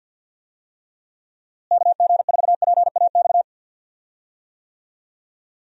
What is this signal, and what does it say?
Morse code, heard as a keyed tone, about 52 wpm, 700 Hz. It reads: KG4PAX
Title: USA Call Signs - 6 Characters - Repeated 2x - 50wpm